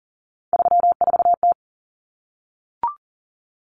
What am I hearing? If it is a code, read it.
Morse code, heard as a keyed tone, about 40 wpm, 700 Hz. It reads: 34T
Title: Common Cut Numbers - 3-Digit 40wpm